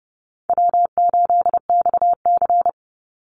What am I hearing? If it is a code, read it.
Morse code, heard as a keyed tone, about 30 wpm, 700 Hz. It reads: W8XC
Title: Easy Call Signs - 30wpm